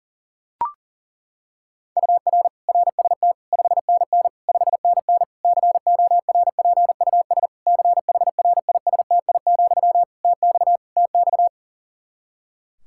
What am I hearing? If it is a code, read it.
Morse code, heard as a keyed tone, about 40 wpm, 700 Hz. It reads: UR RST 5NN 5NN CORPUS CHRISTI, TX TX
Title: K1USN SST QSO Practice 40wpm